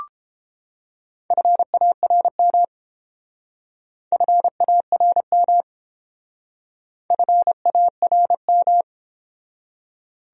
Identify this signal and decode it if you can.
Morse code, heard as a keyed tone, about 30 wpm, 700 Hz. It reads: FARM FARM FARM